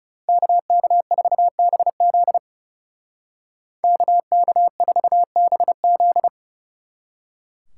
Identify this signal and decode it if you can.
Morse code, heard as a keyed tone, about 36 wpm, 700 Hz. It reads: KK4BZ KK4BZ
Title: Speed Racing - USA 2x2 Call Signs 30wpm